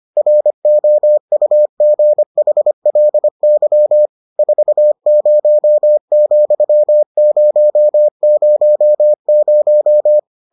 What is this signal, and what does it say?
Morse code, heard as a keyed tone, about 25 wpm, 600 Hz. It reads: ROUGHLY 40,000